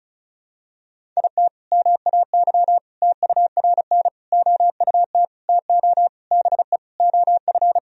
Morse code, heard as a keyed tone, about 34 wpm, 700 Hz. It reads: IT MAY TURN OUT TO BE OF
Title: Sentences from Top 400 Words and Abbreviations 35wpm